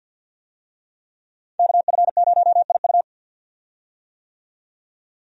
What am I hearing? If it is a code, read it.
Morse code, heard as a keyed tone, about 49 wpm, 700 Hz. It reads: KF0IU
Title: USA Call Signs - 5 Characters - 50wpm